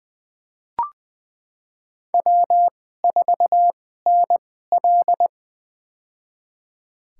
Morse code, heard as a keyed tone, about 20 wpm, spaced as for 14 wpm, 700 Hz. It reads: W4NL
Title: USA Call Signs - 4 Characters - 2x Character Spacing - 20wpm